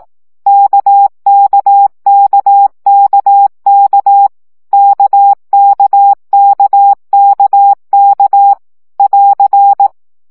Morse code, heard as a keyed tone, about 18 wpm, 800 Hz. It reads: KKKKK KKKKK <AR>
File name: CW gyakorlas k 15 baud